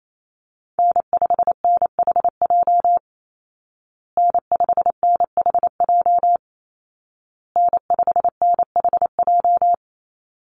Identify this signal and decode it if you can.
Morse code, heard as a keyed tone, about 28 wpm, 700 Hz. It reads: N5NHJ N5NHJ N5NHJ
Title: Call Signs Repeated 2x - 28wpm